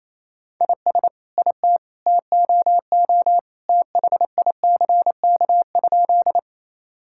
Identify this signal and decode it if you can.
Morse code, heard as a keyed tone, about 28 wpm, 700 Hz. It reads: IS IT TOO THICK?
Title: Sentences from Top 1600 Words 28wpm